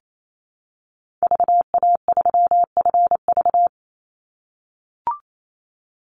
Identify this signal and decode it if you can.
Morse code, heard as a keyed tone, about 28 wpm, 700 Hz. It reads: VA3FV
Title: Call Signs - 28wpm